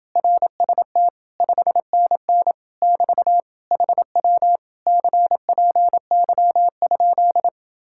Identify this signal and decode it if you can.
Morse code, heard as a keyed tone, about 29 wpm, 700 Hz. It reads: RST 5NN <BT> HW CPY?